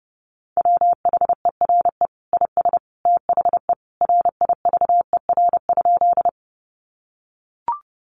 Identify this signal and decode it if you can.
Morse code, heard as a keyed tone, about 30 wpm, 700 Hz. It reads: WHERE IS THE RIVER?
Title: Sentences from Top 300 Words 30wpm